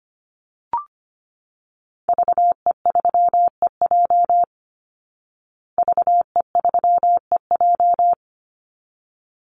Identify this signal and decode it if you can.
Morse code, heard as a keyed tone, about 25 wpm, 700 Hz. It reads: VE3EJ VE3EJ
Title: Call Signs Repeated 3x - 25wpm